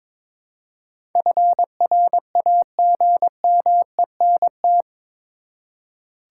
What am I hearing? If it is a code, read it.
Morse code, heard as a keyed tone, about 22 wpm, 700 Hz. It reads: FRAGMENT
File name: practice / 22wpm / Eight Letter Words - 22wpm